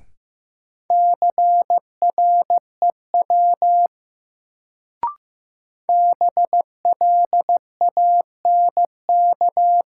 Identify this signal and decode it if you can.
Morse code, heard as a keyed tone, about 15 wpm, 700 Hz. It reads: CREW BLANK
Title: Sentences from Top 1500 Words - New Word Drill Alt - 15wpm